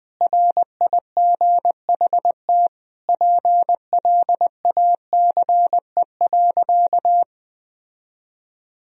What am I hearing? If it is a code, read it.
Morse code, heard as a keyed tone, about 21 wpm, 700 Hz. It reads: RIGHT PLACE.
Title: Sentences from Top 1000 Words - Review Part 3 - 20wpm